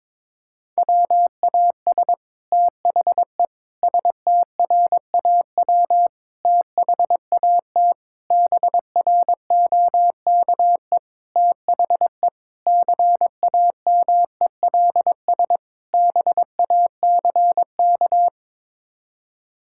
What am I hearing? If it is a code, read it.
Morse code, heard as a keyed tone, about 21 wpm, 700 Hz. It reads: WAS THE STRAW THAT BROKE THE CAMELS BACK
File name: practice / 22wpm / Sets of 8 Words - 22wpm